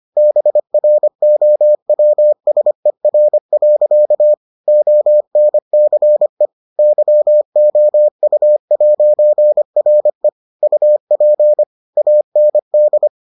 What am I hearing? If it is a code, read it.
Morse code, heard as a keyed tone, about 25 wpm, 600 Hz. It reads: BROWSER. ONCE YOU'RE UP AND